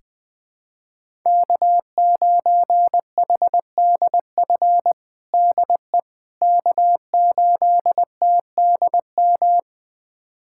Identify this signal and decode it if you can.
Morse code, heard as a keyed tone, about 20 wpm, 700 Hz. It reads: K9HDF DE K8TDM
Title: USA Callsign DE Callsign - 5 Characters - 20wpm